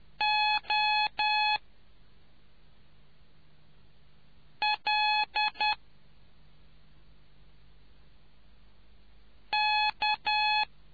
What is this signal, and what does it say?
Morse code, heard as a keyed tone, about 10 wpm, 850 Hz. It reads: O L K